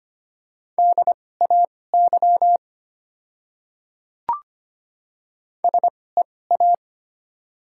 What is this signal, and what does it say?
Morse code, heard as a keyed tone, about 25 wpm, spaced as for 18 wpm, 690 Hz. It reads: DAY SEA
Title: 2x Character Spacing - Top 500 Words - 25wpm